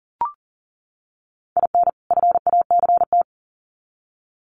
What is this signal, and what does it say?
Morse code, heard as a keyed tone, about 40 wpm, 700 Hz. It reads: IN FACT
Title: Sets of 2 Words 40wpm